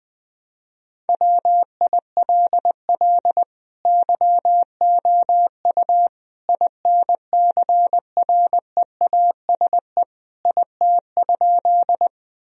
Morse code, heard as a keyed tone, about 21 wpm, 700 Hz. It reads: WILL YOU INCREASE IT?
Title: Sentences from Top 1000 Words - Review Part 4 - 20wpm